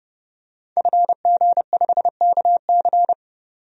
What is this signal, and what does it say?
Morse code, heard as a keyed tone, about 29 wpm, 700 Hz. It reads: FG5KC
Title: DXCC-100 Call Signs - 5 Characters - 30wpm